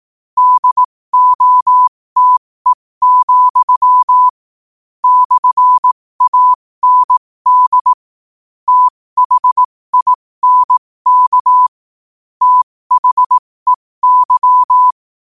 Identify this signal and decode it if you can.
Morse code, heard as a keyed tone, about 18 wpm, spaced as for 15 wpm, 1000 Hz. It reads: DOTE, /AND THINK THEY
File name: qotdfiles / 15wpm